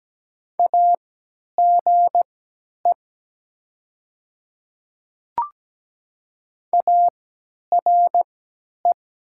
Morse code, heard as a keyed tone, about 17 wpm, spaced as for 10 wpm, 700 Hz. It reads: AGE ARE
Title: 3x Character Spacing - Top 500 Words - 17wpm